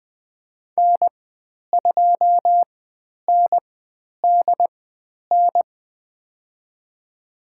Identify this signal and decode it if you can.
Morse code, heard as a keyed tone, about 20 wpm, spaced as for 10 wpm, 700 Hz. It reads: N2NDN
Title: 109 Lesson 18g - Letter m - Call Signs 20wpm